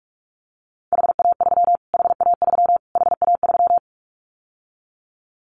Morse code, heard as a keyed tone, about 46 wpm, 700 Hz. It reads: HA3 HA3 HA3
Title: Callsign Prefixes for Top 25 DXCC - Repeated 3x - 45wpm